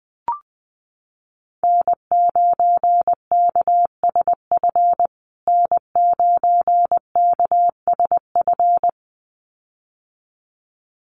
Morse code, heard as a keyed tone, about 20 wpm, 700 Hz. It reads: N9KSF N9KSF
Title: USA Call Signs Repeated 2x 20wpm